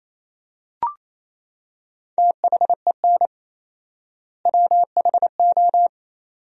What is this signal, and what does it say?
Morse code, heard as a keyed tone, about 28 wpm, 700 Hz. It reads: THEN WHO
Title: 4x Word-Spacing - Sentences from Top 100 Words - 28wpm